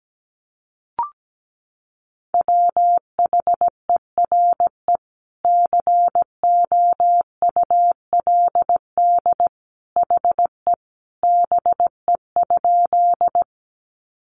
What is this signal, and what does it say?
Morse code, heard as a keyed tone, about 17 wpm, 700 Hz. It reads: WHERE COULD HE BE?
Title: Sentences from Top 200 Words 17wpm